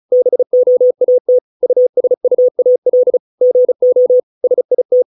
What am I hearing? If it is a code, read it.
Morse code, heard as a keyed tone, about 36 wpm, 500 Hz. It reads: BOAT USUAL GO SIT